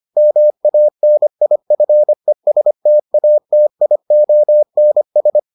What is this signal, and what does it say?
Morse code, heard as a keyed tone, about 25 wpm, 600 Hz. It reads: MANIFESTATIONS